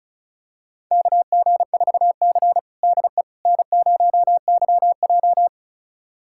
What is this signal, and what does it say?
Morse code, heard as a keyed tone, about 36 wpm, 700 Hz. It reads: KG4C DE N0YJ
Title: USA Callsign DE Callsign - 4 Characters - 35wpm